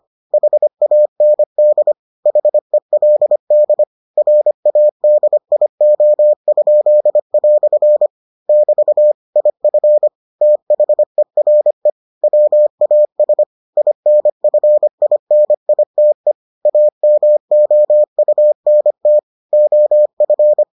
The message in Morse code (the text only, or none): HAND HELD RADIO?" <BT> IF THERE WAS INFINITE AMOUNT OF